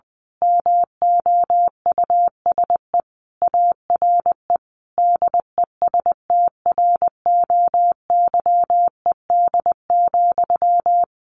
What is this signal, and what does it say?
Morse code, heard as a keyed tone, about 20 wpm, 700 Hz. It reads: MOUSE ARE DESTROYED,
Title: A Sound Of Thunder 20wpm